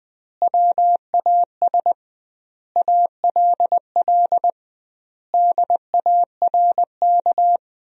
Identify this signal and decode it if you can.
Morse code, heard as a keyed tone, about 20 wpm, 700 Hz. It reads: WAS ALL DARK
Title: 2x Word-Spacing - Sentences from Top 500 Words - 20wpm